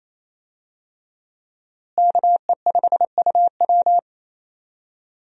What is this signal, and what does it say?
Morse code, heard as a keyed tone, about 28 wpm, 700 Hz. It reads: KE5UW
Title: USA Call Signs - 5 Characters - Repeated 2x - 28wpm